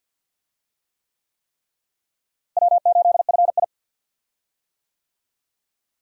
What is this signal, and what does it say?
Morse code, heard as a keyed tone, about 48 wpm, 700 Hz. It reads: W8FI
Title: USA Call Signs - 4 Characters - Repeated 3x - 50wpm